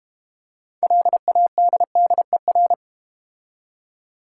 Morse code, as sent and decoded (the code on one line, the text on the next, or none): .-.. .- -.. -.. .-.
LADDER